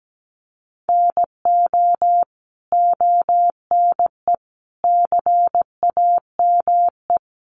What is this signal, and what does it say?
Morse code, heard as a keyed tone, about 17 wpm, 700 Hz. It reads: NO ONE CAME
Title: Sentences from Top 1300 Words 17wpm